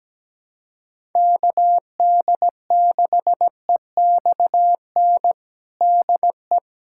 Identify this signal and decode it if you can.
Morse code, heard as a keyed tone, about 17 wpm, 700 Hz. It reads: KD6EXN DE
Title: USA Callsign DE Callsign 17wpm